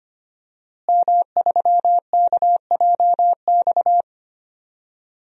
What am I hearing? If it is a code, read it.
Morse code, heard as a keyed tone, about 25 wpm, 700 Hz. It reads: M3KJX
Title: DXCC-100 Call Signs - 5 Characters - Repeated 3x - 25wpm